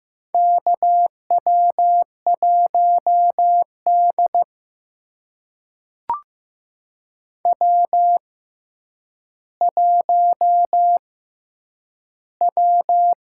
Morse code, heard as a keyed tone, about 15 wpm, spaced as for 8 wpm, 700 Hz. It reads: KW1D W1W
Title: USA Call Signs - 4 Characters - 6x Character Spacing - 15wpm